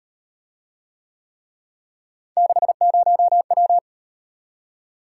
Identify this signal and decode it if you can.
Morse code, heard as a keyed tone, about 38 wpm, 700 Hz. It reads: 60W